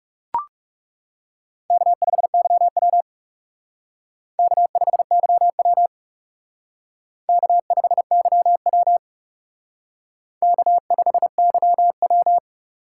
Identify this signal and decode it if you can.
Morse code, heard as a keyed tone, about 37 wpm, 700 Hz. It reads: K5YW K5YW K5YW K5YW
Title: Speed Racing - USA 1x2 Call Signs 30wpm